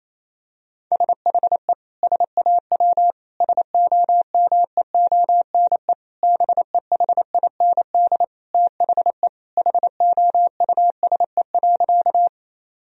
Morse code, heard as a keyed tone, about 28 wpm, 700 Hz. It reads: SHE SAW SOMEONE BEHIND THE HOUSE.
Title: Sentences from Top 1000 Words - Review Part 4 - 28wpm